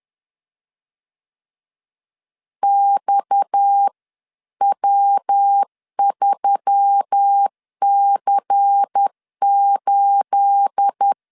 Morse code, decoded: XW3C8

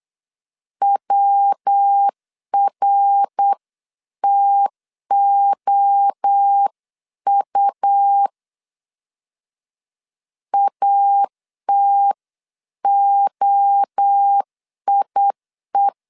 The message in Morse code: WRTOU ATOIE

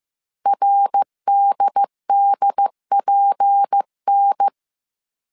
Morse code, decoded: RDDPN

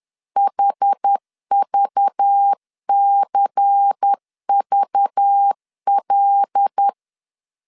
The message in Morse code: HVCVL

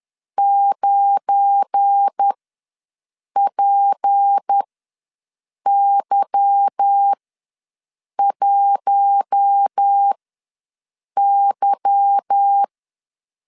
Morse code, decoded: 9PY1Y

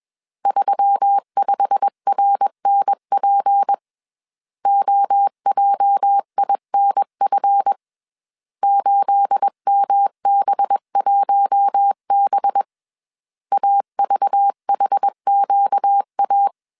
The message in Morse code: 35RNP OJINF 8MB1B AVHQA

